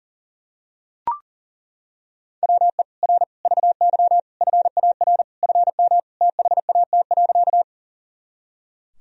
Morse code, heard as a keyed tone, about 40 wpm, 700 Hz. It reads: WE R VY FAR FM THAT.
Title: Sentences from Top 300 Words and Abbreviations 40wpm